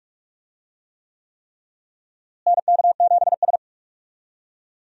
Morse code, heard as a keyed tone, about 45 wpm, 700 Hz. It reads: NK7S
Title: USA Call Signs - 4 Characters - Repeated 3x - 45wpm